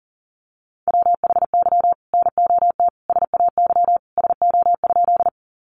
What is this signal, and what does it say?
Morse code, heard as a keyed tone, about 40 wpm, 700 Hz. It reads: WHY NOT SAY SO?
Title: Sentences from Top 600 Words 40wpm